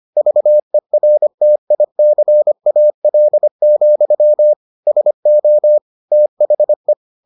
Morse code, heard as a keyed tone, about 25 wpm, 600 Hz. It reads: VERTICAL, SO THE